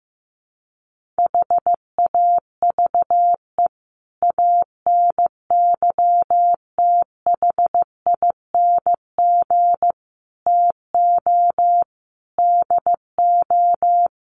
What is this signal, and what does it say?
Morse code, heard as a keyed tone, about 15 wpm, 700 Hz. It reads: HAVE ANYTHING TO DO
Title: Sets of 5 Words 15wpm